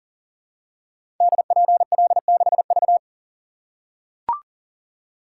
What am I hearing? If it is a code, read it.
Morse code, heard as a keyed tone, about 39 wpm, 700 Hz. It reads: DPL6V
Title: Code Groups - 5 Letter:Numbers - 40wpm